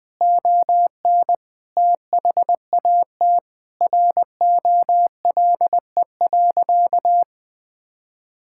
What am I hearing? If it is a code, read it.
Morse code, heard as a keyed tone, about 20 wpm, 700 Hz. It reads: ON THAT ROLE.
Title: Sentences from Top 1000 Words - Review Part 6 - 20wpm